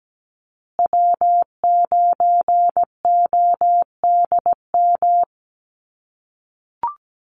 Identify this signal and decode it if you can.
Morse code, heard as a keyed tone, about 17 wpm, 700 Hz. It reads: W9ODM
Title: USA Call Signs Repeated 3x 17wpm